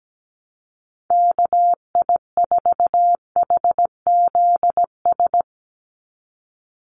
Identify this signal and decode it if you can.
Morse code, heard as a keyed tone, about 17 wpm, 700 Hz. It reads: KI4HZS
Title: USA Call Signs 17wpm